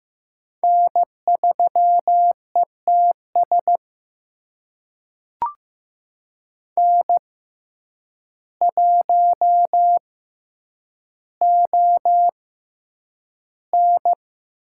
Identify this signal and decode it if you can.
Morse code, heard as a keyed tone, about 15 wpm, spaced as for 8 wpm, 700 Hz. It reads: N3ETS N1ON
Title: USA Call Signs - 5 Characters - 6x Character Spacing - 15wpm